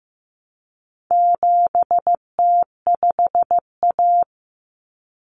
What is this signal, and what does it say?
Morse code, heard as a keyed tone, about 15 wpm, 700 Hz. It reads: 7T5A
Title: Common Cut Numbers - 4-Digit 15wpm